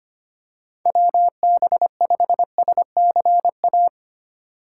Morse code, heard as a keyed tone, about 25 wpm, 700 Hz. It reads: WB5SCA
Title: USA Call Signs - 6 Characters - 2x Character Spacing - 25wpm